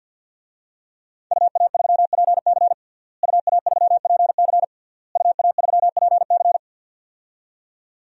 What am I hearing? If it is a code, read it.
Morse code, heard as a keyed tone, about 50 wpm, 700 Hz. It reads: UA3PC UA3PC UA3PC
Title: DXCC-100 Call Signs - 5 Characters - Repeated 3x - 50wpm